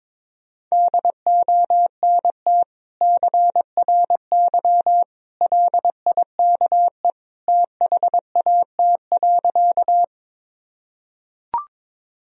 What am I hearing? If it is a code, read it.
Morse code, heard as a keyed tone, about 22 wpm, 700 Hz. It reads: DONT CRY LIKE THAT.
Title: Sentences from Top 1000 Words - Review Part 2 - 22wpm